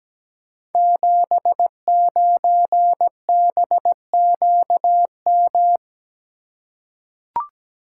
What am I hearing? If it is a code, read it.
Morse code, heard as a keyed tone, about 17 wpm, 700 Hz. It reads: 79BQM
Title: Code Groups - 5 Letter:Numbers - 17wpm